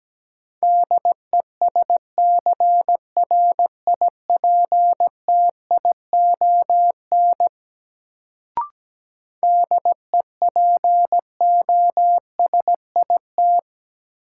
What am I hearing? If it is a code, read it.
Morse code, heard as a keyed tone, about 17 wpm, 700 Hz. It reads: DESCRIPTION DEPOSIT
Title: Sentences from Top 1300 Words - New Word Drill - 17wpm